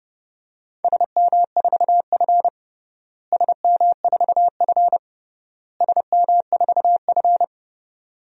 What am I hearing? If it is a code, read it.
Morse code, heard as a keyed tone, about 30 wpm, 700 Hz. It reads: SM4F SM4F SM4F
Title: DXCC-100 Call Signs - 4 Characters - Repeated 3x - 30wpm